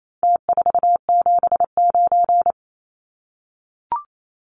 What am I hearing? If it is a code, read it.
Morse code, heard as a keyed tone, about 28 wpm, 700 Hz. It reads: T479
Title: Numbers - Contest Style - 4-Digit 28wpm